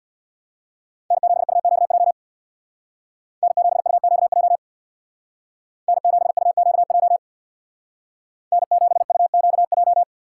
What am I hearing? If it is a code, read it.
Morse code, heard as a keyed tone, about 64 wpm, 700 Hz. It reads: N7UQJ N7UQJ N7UQJ N7UQJ